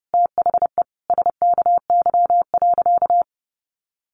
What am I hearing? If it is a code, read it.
Morse code, heard as a keyed tone, about 30 wpm, 700 Hz. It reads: THE SKY.
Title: A Sound Of Thunder 30wpm